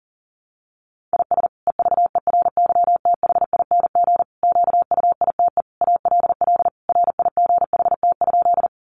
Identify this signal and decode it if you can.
Morse code, heard as a keyed tone, about 40 wpm, 700 Hz. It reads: IS EVERYTHING QUITE ALL RIGHT?
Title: Sentences from Top 700 Words 40wpm